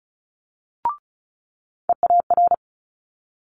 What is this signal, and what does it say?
Morse code, heard as a keyed tone, about 36 wpm, 700 Hz. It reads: EAR
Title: Multi-letter Phonemes Quick Fire 35wpm